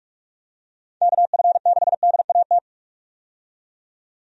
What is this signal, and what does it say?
Morse code, heard as a keyed tone, about 46 wpm, 700 Hz. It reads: KF6DAT
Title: USA Call Signs - 6 Characters - 45wpm